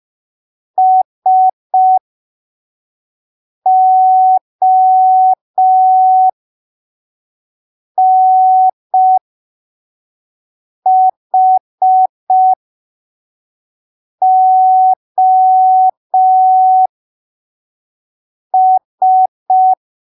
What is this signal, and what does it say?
Morse code, heard as a keyed tone, about 5 wpm, spaced as for 3 wpm, 750 Hz. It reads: SONHOS
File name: arquivos / texto / texto 15 05ppm